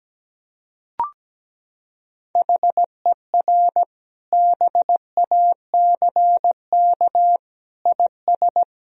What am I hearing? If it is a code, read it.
Morse code, heard as a keyed tone, about 17 wpm, 700 Hz. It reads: HER BACK IS
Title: Sentences from Top 1000 Words - Review Part 1 - 17wpm